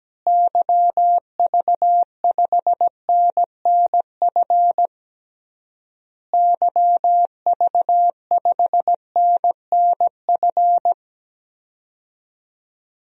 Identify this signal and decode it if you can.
Morse code, heard as a keyed tone, about 17 wpm, 700 Hz. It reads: YV5NNF YV5NNF
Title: DXCC-100 Call Signs - 6 Characters - Repeated 3x - 17wpm